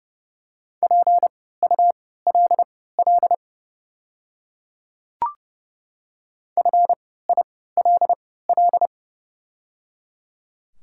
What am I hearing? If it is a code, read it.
Morse code, heard as a keyed tone, about 30 wpm, spaced as for 17 wpm, 690 Hz. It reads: PULL FILL
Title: 3x Character Spacing - Top 500 Words - 30wpm